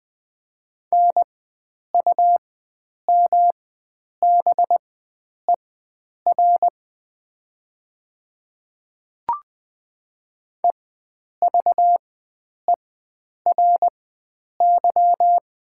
Morse code, heard as a keyed tone, about 20 wpm, spaced as for 9 wpm, 700 Hz. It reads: NUMBER EVERY
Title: 4x Character Spacing - Top 300 Words - 20wpm